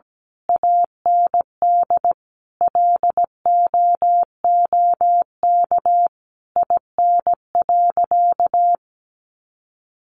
Morse code, heard as a keyed tone, about 17 wpm, 700 Hz. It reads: AND LOOK IN.
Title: Sentences from Top 1300 Words 17wpm